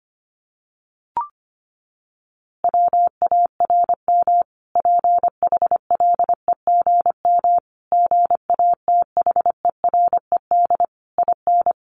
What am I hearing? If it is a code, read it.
Morse code, heard as a keyed tone, about 25 wpm, 700 Hz. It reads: WARM PHLEGM GATHERED IN
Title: A Sound Of Thunder 25wpm